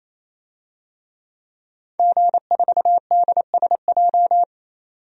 Morse code, heard as a keyed tone, about 27 wpm, 700 Hz. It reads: G4DSJ
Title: DXCC-100 Call Signs - 5 Characters - 28wpm